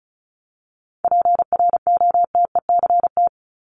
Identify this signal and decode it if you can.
Morse code, heard as a keyed tone, about 34 wpm, 700 Hz. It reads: PROTECT
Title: Top 751-1000 Words 35wpm